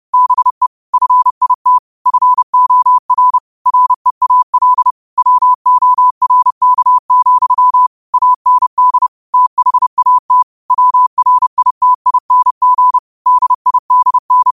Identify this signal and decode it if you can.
Morse code, heard as a keyed tone, about 30 wpm, 1000 Hz. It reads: BE FIT FOR REAL WORK, AND THAT WRITING DIDN
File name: qotdfiles / 30wpm